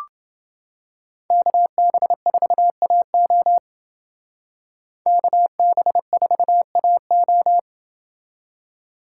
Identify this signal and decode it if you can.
Morse code, heard as a keyed tone, about 29 wpm, 700 Hz. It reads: KB4AO KB4AO